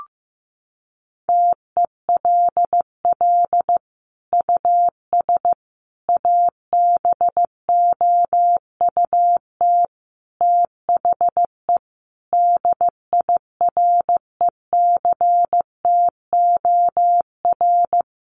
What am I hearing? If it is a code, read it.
Morse code, heard as a keyed tone, about 15 wpm, 700 Hz. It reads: TELL US ABOUT THE DIRECTOR
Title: Sentences from Top 1300 Words 15wpm